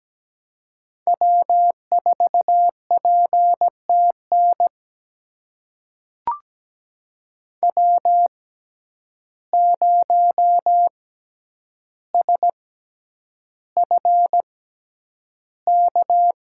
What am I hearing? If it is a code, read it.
Morse code, heard as a keyed tone, about 17 wpm, spaced as for 9 wpm, 700 Hz. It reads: W4PTN W0SFK
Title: USA Call Signs - 5 Characters - 6x Character Spacing - 17wpm